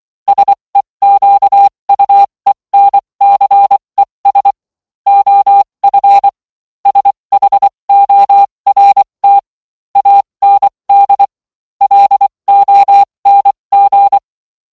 Morse code, heard as a keyed tone, about 24 wpm, spaced as for 20 wpm, 800 Hz. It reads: SEQUENCES OF SHORT AND LONG